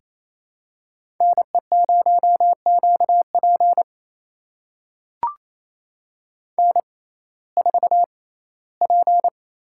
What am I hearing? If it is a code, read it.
Morse code, heard as a keyed tone, about 28 wpm, spaced as for 17 wpm, 700 Hz. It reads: NE0QP N4P
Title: USA Call Signs - 6x Character Spacing - 28wpm